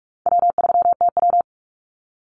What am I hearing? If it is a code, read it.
Morse code, heard as a keyed tone, about 44 wpm, 700 Hz. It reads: W3TW
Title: Easy Call Signs - 45wpm